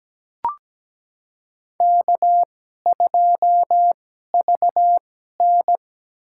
Morse code, heard as a keyed tone, about 17 wpm, spaced as for 12 wpm, 700 Hz. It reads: K2VN
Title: USA Call Signs - 2x Character Spacing - 17wpm